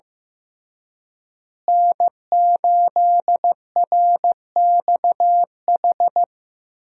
Morse code, heard as a keyed tone, about 15 wpm, 700 Hz. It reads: N8RXH
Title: USA Call Signs - 5 Characters - Repeated 3x - 15wpm